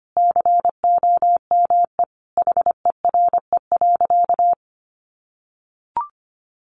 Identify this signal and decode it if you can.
Morse code, heard as a keyed tone, about 25 wpm, 700 Hz. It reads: COME HERE.
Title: Sentences from Top 1500 Words 25wpm